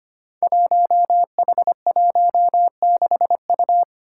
Morse code, heard as a keyed tone, about 25 wpm, 700 Hz. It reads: 1H16U
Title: Five Challenging Letter-Number Combinations 25wpm